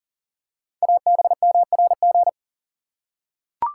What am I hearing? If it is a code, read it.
Morse code, heard as a keyed tone, about 38 wpm, 690 Hz. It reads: ABMRG
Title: Code Groups - 5 Letters - 40wpm